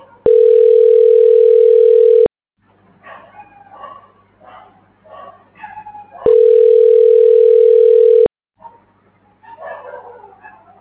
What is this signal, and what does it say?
Ringback tone (North America), 440 + 480 Hz.